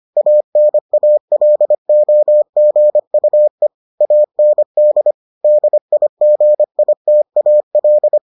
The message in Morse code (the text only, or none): ANALOGUE AND DIGITAL